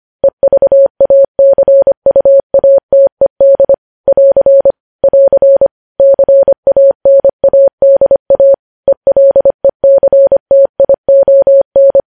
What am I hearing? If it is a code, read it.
Morse code, heard as a keyed tone, about 25 wpm, 550 Hz. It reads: EVACUATED <AR> <AR> CANADA ELECTION
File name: podcast / News headlines-1x-25x25